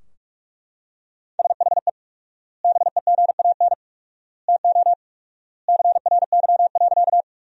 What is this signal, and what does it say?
Morse code, heard as a keyed tone, about 41 wpm, 700 Hz. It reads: SHE BEGAN TO CRY.
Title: 4x Word-Spacing - Sentences from Top 500 Words - 45wpm